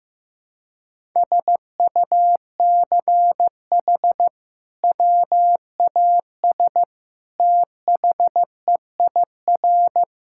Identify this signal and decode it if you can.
Morse code, heard as a keyed tone, about 16 wpm, 700 Hz. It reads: SUCH WAS THEIR
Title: Sentences from Top 1000 Words - Review Part 5 - 15wpm